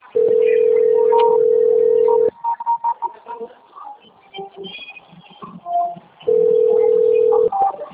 Ringback tone (North America), 440 + 480 Hz.
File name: out- - - - - .11582